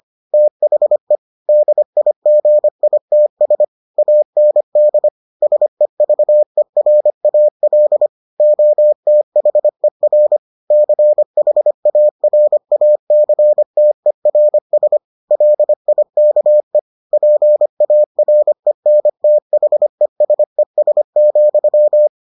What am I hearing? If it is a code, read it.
Morse code, heard as a keyed tone, about 25 wpm, 600 Hz. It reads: THE DIGITS AND SEVERAL OTHER CHARACTERS LIKE PARENTHESES,